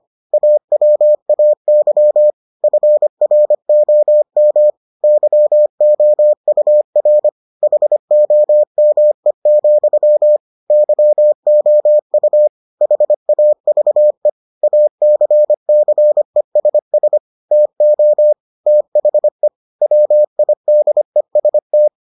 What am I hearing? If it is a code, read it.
Morse code, heard as a keyed tone, about 25 wpm, 600 Hz. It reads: AWAY FROM YOUR HOME, YOU HAVE ACCESS TO THE WIDEST